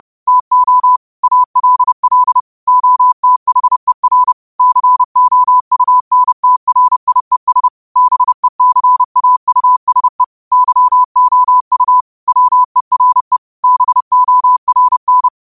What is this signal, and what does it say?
Morse code, heard as a keyed tone, about 30 wpm, 1000 Hz. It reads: TO ALL OTHER COUNTRIES BECAUSE YOU WERE BORN